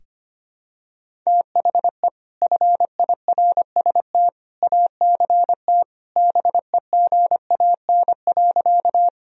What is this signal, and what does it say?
Morse code, heard as a keyed tone, about 25 wpm, 700 Hz. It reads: THE FIRST ACT BEGAN.
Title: Sentences from Top 1000 Words - Review Part 1 - 25wpm